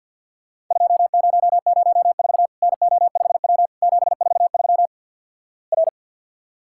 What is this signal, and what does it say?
Morse code, heard as a keyed tone, about 53 wpm, 700 Hz. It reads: 2004 NO5W 743 R
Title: Worked All Europe DX Contest - QTC Exchange - 50wpm